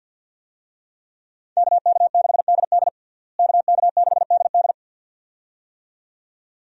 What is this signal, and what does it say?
Morse code, heard as a keyed tone, about 47 wpm, 700 Hz. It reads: KK6DD KK6DD
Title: USA Call Signs - 5 Characters - Repeated 2x - 50wpm